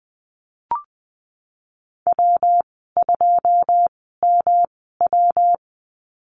Morse code, heard as a keyed tone, about 20 wpm, spaced as for 15 wpm, 700 Hz. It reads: W2MW
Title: 179 Lesson 29g - Slash - Call Signs 20wpm